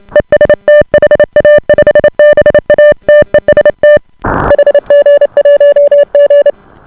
Morse code, heard as a keyed tone, about 28 wpm, 590 Hz. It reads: EST HA5BA TEST HG1G